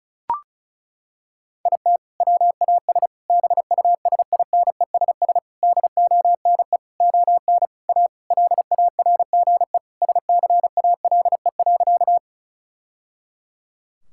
Morse code, heard as a keyed tone, about 36 wpm, 700 Hz. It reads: IT WAS BUSINESS DONE ON A LARGE SCALE.
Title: Sentences from Top 1000 Words - Review Part 5 - 35wpm